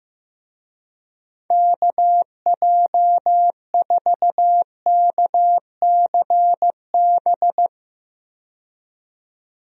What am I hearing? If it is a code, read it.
Morse code, heard as a keyed tone, about 15 wpm, 700 Hz. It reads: KJ4KCB
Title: USA Call Signs - 6 Characters - 15wpm